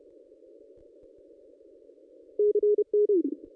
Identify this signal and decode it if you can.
Morse code, heard as a keyed tone, about 26 wpm, 440 Hz. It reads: CN
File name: contests / ss-cw-k5tr-op-n5rz -1400